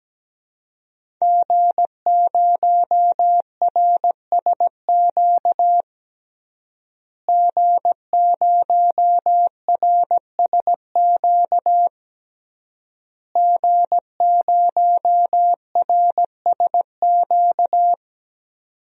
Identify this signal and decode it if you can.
Morse code, heard as a keyed tone, about 17 wpm, 700 Hz. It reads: G0RSQ G0RSQ G0RSQ
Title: DXCC-100 Call Signs - 5 Characters - Repeated 3x - 17wpm